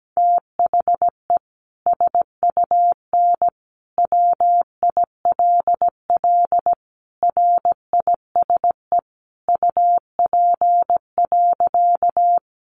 Morse code, heard as a keyed tone, about 17 wpm, 700 Hz. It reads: THE SUN WILL RISE UP.
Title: Sentences from Top 900 Words 17wpm